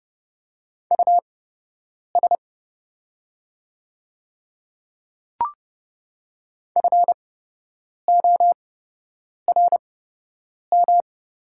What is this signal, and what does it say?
Morse code, heard as a keyed tone, about 30 wpm, spaced as for 8 wpm, 700 Hz. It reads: US FORM
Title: 8x Character Spacing - Top 200 Words - 30wpm